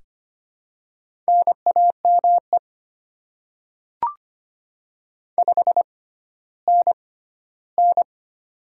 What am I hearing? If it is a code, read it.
Morse code, heard as a keyed tone, about 25 wpm, spaced as for 14 wpm, 690 Hz. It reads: NAME 5NN
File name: Top 20 QSO Elements - 6x Character Spacing - 25wpm